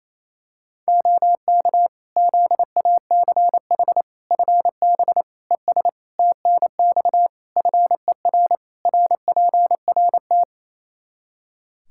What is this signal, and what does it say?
Morse code, heard as a keyed tone, about 27 wpm, 700 Hz. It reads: OK ZACH FB ES TNX FER RPRT
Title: K1USN SST QSO Practice 28wpm